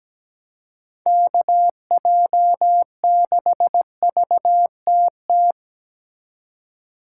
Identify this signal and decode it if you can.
Morse code, heard as a keyed tone, about 20 wpm, 700 Hz. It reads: KJ6VTT
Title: Speed Racing - USA 2x3 Call Signs 17wpm